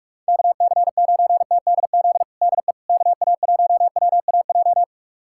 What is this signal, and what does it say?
Morse code, heard as a keyed tone, about 46 wpm, 700 Hz. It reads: KC9TDZ DE KA1WAJ
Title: USA Callsign DE Callsign - 6 Characters - 45wpm